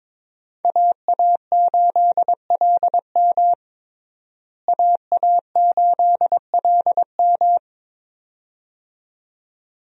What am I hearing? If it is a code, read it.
Morse code, heard as a keyed tone, about 22 wpm, 700 Hz. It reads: AA8LM AA8LM
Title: USA Call Signs - 5 Characters - Repeated 2x - 22wpm